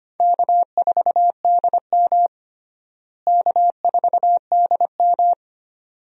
Morse code, heard as a keyed tone, about 25 wpm, 700 Hz. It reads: K4DM K4DM
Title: USA Call Signs - 4 Characters - Repeated 2x - 25wpm